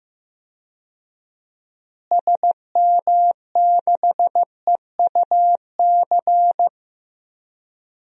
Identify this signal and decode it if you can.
Morse code, heard as a keyed tone, about 15 wpm, 700 Hz. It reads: SM6EUC
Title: DXCC-100 Call Signs - 6 Characters - Repeated 2x - 15wpm